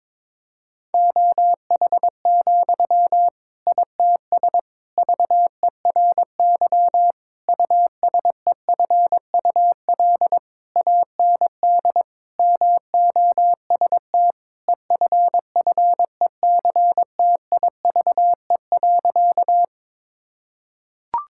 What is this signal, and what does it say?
Morse code, heard as a keyed tone, about 22 wpm, 700 Hz. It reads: OH, ITS VERY USEFUL AND MOST EFFECTIVE.
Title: Sentences from Top 1000 Words - Review Part 6 - 22wpm